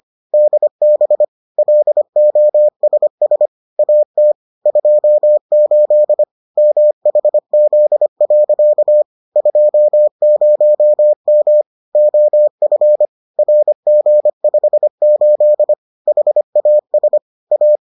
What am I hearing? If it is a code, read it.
Morse code, heard as a keyed tone, about 25 wpm, 600 Hz. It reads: DB LOSS AT 28 MHZ. 20M OF RG58 HAS A